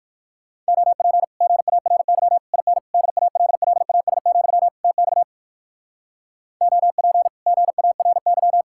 Morse code, heard as a keyed tone, about 57 wpm, 700 Hz. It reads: OP GARY IN DALLAS, TX OP GARY